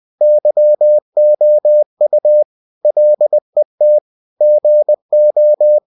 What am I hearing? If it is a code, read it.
Morse code, heard as a keyed tone, about 20 wpm, 600 Hz. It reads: YOU LET GO
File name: cw-books / winnie-the-pooh / winnie-20wpm-page-0006